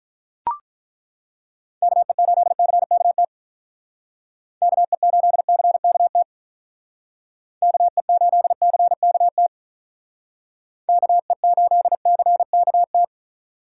Morse code, heard as a keyed tone, about 41 wpm, 700 Hz. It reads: KE8CKT KE8CKT KE8CKT KE8CKT
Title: Speed Racing - USA Call Signs 35wpm